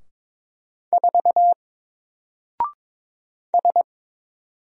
Morse code, heard as a keyed tone, about 22 wpm, 700 Hz. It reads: 4 S
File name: Single-Letter-Number-22wpm